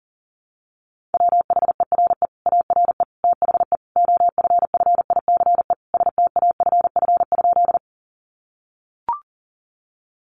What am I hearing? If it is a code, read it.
Morse code, heard as a keyed tone, about 40 wpm, 700 Hz. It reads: WHERE ARE THE OFFICE STAFF?
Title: Sentences from Top 1200 Words 40wpm